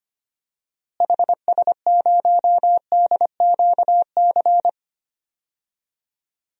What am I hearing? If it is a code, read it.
Morse code, heard as a keyed tone, about 25 wpm, 700 Hz. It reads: HS0DQC
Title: DXCC-100 Call Signs - 6 Characters - Repeated 2x - 25wpm